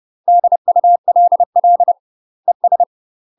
Morse code, heard as a keyed tone, about 30 wpm, 700 Hz. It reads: DULL ES
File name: UK MASTER QSOs TRAINING/30WPM